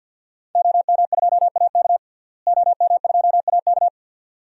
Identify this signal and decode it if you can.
Morse code, heard as a keyed tone, about 52 wpm, 700 Hz. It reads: OM2AK OM2AK
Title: DXCC-100 Call Signs - 5 Characters - Repeated 2x - 50wpm